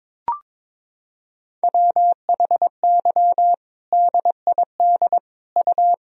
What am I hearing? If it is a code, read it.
Morse code, heard as a keyed tone, about 22 wpm, 700 Hz. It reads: WHY DID U
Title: Sentences from Top 500 Words and Abbreviations 22wpm